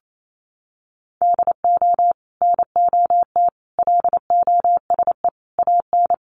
Morse code, heard as a keyed tone, about 28 wpm, 700 Hz. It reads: DO NOT LOSE AN
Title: Sentences from Top 1100 Words 28wpm